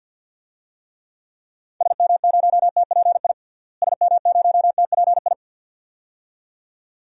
Morse code, heard as a keyed tone, about 47 wpm, 700 Hz. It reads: SM0TPI SM0TPI
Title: DXCC-100 Call Signs - 6 Characters - Repeated 2x - 50wpm